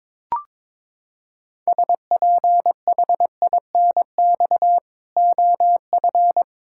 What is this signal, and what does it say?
Morse code, heard as a keyed tone, about 22 wpm, 700 Hz. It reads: SPHINX OF
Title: Pangram Challenge v2 - 22wpm